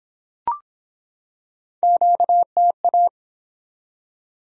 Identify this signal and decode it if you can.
Morse code, heard as a keyed tone, about 23 wpm, 700 Hz. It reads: QTA